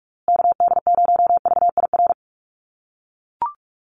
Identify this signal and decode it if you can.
Morse code, heard as a keyed tone, about 44 wpm, 700 Hz. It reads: KD0VIR
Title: USA Call Signs Repeated 3x 45wpm